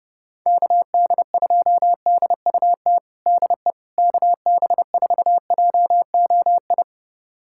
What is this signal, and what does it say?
Morse code, heard as a keyed tone, about 30 wpm, 700 Hz. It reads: KD2DUT DE KB4JOI
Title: USA Callsign DE Callsign - 6 Characters - 30wpm